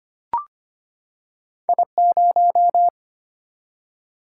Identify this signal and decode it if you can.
Morse code, heard as a keyed tone, about 25 wpm, 700 Hz. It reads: I0
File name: Two Letter-Number Combinations - 25wpm